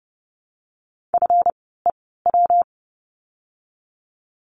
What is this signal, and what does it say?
Morse code, heard as a keyed tone, about 30 wpm, spaced as for 17 wpm, 700 Hz. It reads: FEW
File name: course / 30:10wpm / 230 Lesson 37f - Letter x - Words 30wpm